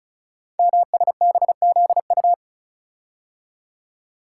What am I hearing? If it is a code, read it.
Morse code, heard as a keyed tone, about 36 wpm, 690 Hz. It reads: MSBZU
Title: Code Groups - 5 Letters - 35wpm